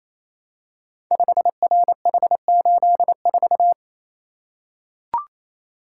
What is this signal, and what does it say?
Morse code, heard as a keyed tone, about 27 wpm, 700 Hz. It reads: 5RH84
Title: Code Groups - 5 Letter:Numbers - 28wpm